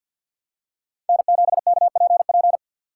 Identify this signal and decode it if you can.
Morse code, heard as a keyed tone, about 53 wpm, 700 Hz. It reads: N7KPP
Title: USA Call Signs - 5 Characters - 2x Character Spacing - 50wpm